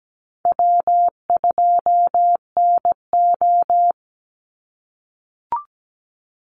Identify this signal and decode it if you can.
Morse code, heard as a keyed tone, about 17 wpm, 700 Hz. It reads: W2NO
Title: Easy Call Signs - 17wpm